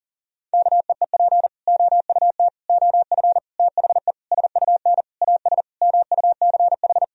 Morse code, heard as a keyed tone, about 40 wpm, 700 Hz. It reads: KEEP OUT OF THE SUN AS MUCH
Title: Sentences from Top 1000 Words - Review Part 1 - 40wpm